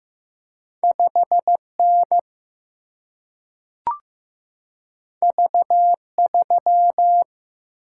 Morse code, heard as a keyed tone, about 15 wpm, 700 Hz. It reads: 5N V3
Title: Single-Letter Single-Number 15wpm